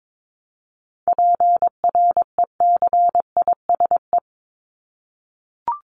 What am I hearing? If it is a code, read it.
Morse code, heard as a keyed tone, about 22 wpm, 700 Hz. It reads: PRECISE
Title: LY Suffix - Top 100 Words with Base Words 22wpm